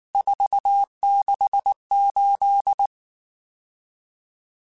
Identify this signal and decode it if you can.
Morse code, heard as a keyed tone, about 19 wpm, 800 Hz. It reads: 468
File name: RSTs / RST-mp3s / Various RST @18